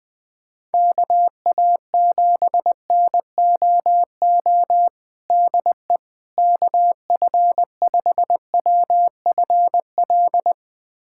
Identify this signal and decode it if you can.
Morse code, heard as a keyed tone, about 20 wpm, 700 Hz. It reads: KA7NOO DE KF5WFL
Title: USA Callsign DE Callsign 20wpm